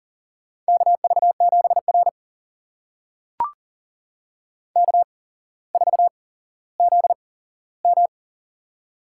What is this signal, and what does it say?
Morse code, heard as a keyed tone, about 42 wpm, spaced as for 17 wpm, 700 Hz. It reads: KV7R K4ZM
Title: USA Call Signs - 4 Characters - 8x Character Spacing - 40wpm